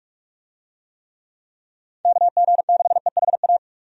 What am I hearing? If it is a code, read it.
Morse code, heard as a keyed tone, about 42 wpm, 700 Hz. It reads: KG6EHA
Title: USA Call Signs - 6 Characters - 45wpm